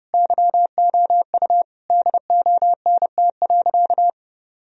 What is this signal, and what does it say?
Morse code, heard as a keyed tone, about 32 wpm, 700 Hz. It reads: YOU DONT.
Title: Sentences from Top 1000 Words - Review Part 4 - 30wpm